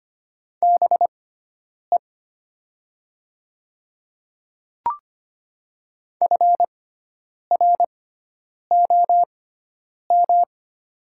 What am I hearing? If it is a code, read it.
Morse code, heard as a keyed tone, about 25 wpm, spaced as for 9 wpm, 700 Hz. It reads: BE FROM
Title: 6x Character Spacing - Top 200 Words - 25wpm